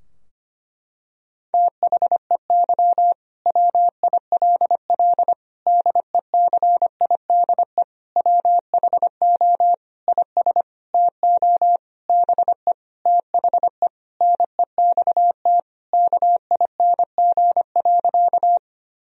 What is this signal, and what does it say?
Morse code, heard as a keyed tone, about 25 wpm, 700 Hz. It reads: THEY WILL DECIDE WHO IS TO BE THE NEXT KING.
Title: Sentences from Top 1000 Words - Review Part 3 - 25wpm